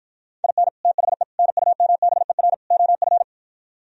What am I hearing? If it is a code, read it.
Morse code, heard as a keyed tone, about 53 wpm, 700 Hz. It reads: IN THE NUMBER OF